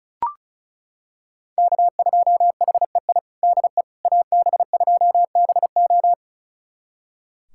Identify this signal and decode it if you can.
Morse code, heard as a keyed tone, about 36 wpm, 700 Hz. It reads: K2HEI DE AB2BO
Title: USA Callsign DE Callsign - 5 Characters - 35wpm